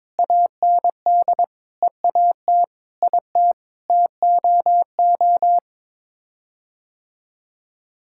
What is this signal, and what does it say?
Morse code, heard as a keyed tone, about 22 wpm, 700 Hz. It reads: AND EAT IT TOO
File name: Sets of 9 Words - 22wpm